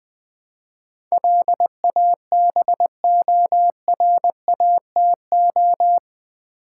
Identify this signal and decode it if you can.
Morse code, heard as a keyed tone, about 20 wpm, 700 Hz. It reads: LABORATO
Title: Build A Word - 4 Letters At A Time - 10-Letter Words - 20wpm